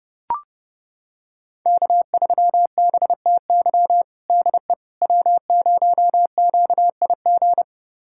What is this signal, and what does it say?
Morse code, heard as a keyed tone, about 32 wpm, 700 Hz. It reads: K3BTY DE W0QIG
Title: USA Callsign DE Callsign - 5 Characters - 30wpm